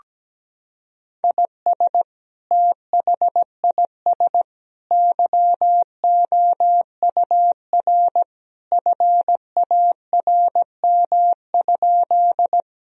Morse code, heard as a keyed tone, about 17 wpm, 700 Hz. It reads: IS THIS YOUR FARM?
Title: Sentences from Top 1000 Words - Review Part 2 - 17wpm